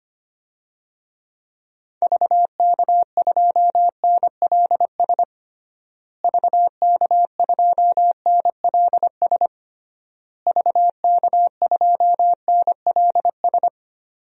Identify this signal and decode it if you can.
Morse code, heard as a keyed tone, about 25 wpm, 700 Hz. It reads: VK2NLS VK2NLS VK2NLS
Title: DXCC-100 Call Signs - 6 Characters - Repeated 3x - 25wpm